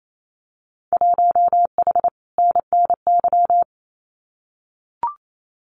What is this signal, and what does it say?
Morse code, heard as a keyed tone, about 28 wpm, 700 Hz. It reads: 1H NNY
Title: Winter Field Day Exchanges 28wpm